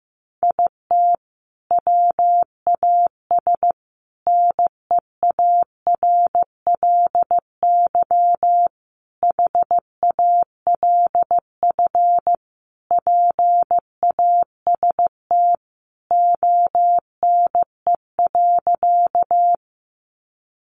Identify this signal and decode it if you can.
Morse code, heard as a keyed tone, about 15 wpm, 700 Hz. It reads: IT WAS NEARLY HALF PAST ONE.
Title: Sentences from Top 1300 Words 15wpm